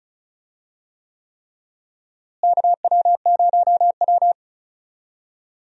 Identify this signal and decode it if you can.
Morse code, heard as a keyed tone, about 36 wpm, 700 Hz. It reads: KW0W
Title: USA Call Signs - 4 Characters - Repeated 3x - 35wpm